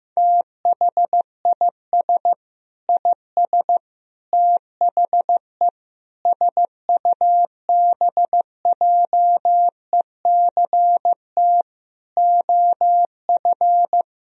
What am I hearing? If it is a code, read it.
Morse code, heard as a keyed tone, about 14 wpm, 700 Hz. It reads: THIS IS THE SUBJECT OF
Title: Sentences from Top 1000 Words - Review Part 4 - 15wpm